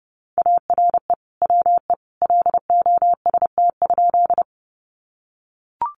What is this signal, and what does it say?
Morse code, heard as a keyed tone, about 30 wpm, 700 Hz. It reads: ARE WE LOST?
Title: Sentences from Top 1400 Words 30wpm